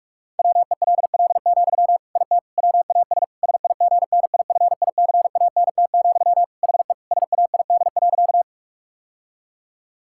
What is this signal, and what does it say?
Morse code, heard as a keyed tone, about 46 wpm, 700 Hz. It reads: WELL, IT WAS SIGNIFICANT, HE SAID.
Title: Sentences from Top 1000 Words - Review Part 6 - 45wpm